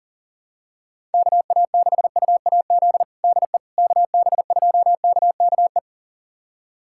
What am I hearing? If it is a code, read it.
Morse code, heard as a keyed tone, about 40 wpm, 700 Hz. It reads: KA6UAZ DE KB2KKE
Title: USA Callsign DE Callsign - 6 Characters - 40wpm